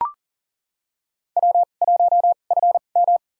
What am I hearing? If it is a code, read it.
Morse code, heard as a keyed tone, about 40 wpm, spaced as for 29 wpm, 700 Hz. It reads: W1FM
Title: USA Call Signs - 2x Character Spacing - 40wpm